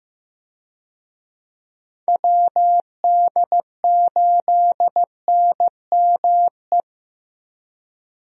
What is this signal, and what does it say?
Morse code, heard as a keyed tone, about 15 wpm, 700 Hz. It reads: WD8NME
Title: USA Call Signs - 6 Characters - Repeated 2x - 15wpm